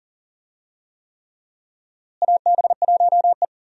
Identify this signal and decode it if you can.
Morse code, heard as a keyed tone, about 40 wpm, 700 Hz. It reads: AB1E
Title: USA Call Signs - 4 Characters - Repeated 3x - 40wpm